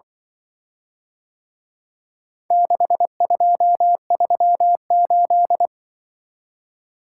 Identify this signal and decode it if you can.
Morse code, heard as a keyed tone, about 24 wpm, 700 Hz. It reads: 6238
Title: Speed-Racing - Four Letter-Number Combinations - 20wpm